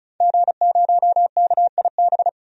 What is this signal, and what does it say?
Morse code, heard as a keyed tone, about 36 wpm, 700 Hz. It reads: G0KIB
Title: DXCC-100 Call Signs - 5 Characters - 35wpm